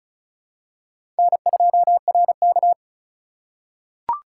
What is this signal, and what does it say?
Morse code, heard as a keyed tone, about 36 wpm, 700 Hz. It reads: N2RK
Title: USA Call Signs - 4 Characters - 4x Character Spacing - 35wpm